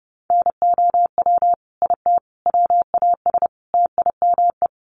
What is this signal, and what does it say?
Morse code, heard as a keyed tone, about 28 wpm, 700 Hz. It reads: NOW IT WAS TIME
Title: Sentences from Top 200 Words 30wpm